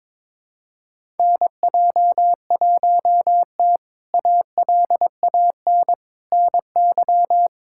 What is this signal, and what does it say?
Morse code, heard as a keyed tone, about 22 wpm, 700 Hz. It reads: NJ1T ALAN NY
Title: K1USN SST QSO Practice - 22wpm